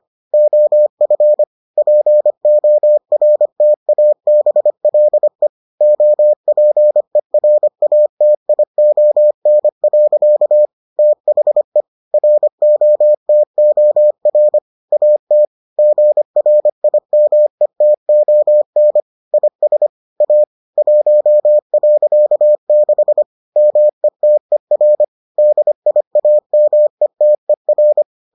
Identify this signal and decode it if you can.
Morse code, heard as a keyed tone, about 25 wpm, 600 Hz. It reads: OF PORTABLE OPERATION. THE ROTOR AT GRIMETON IS A 1.6 METER DIAMETER